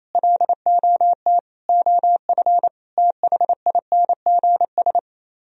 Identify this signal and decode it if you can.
Morse code, heard as a keyed tone, about 34 wpm, 700 Hz. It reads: LOT OF THINGS
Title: Speed-Racing - Sets of 4 Words 28wpm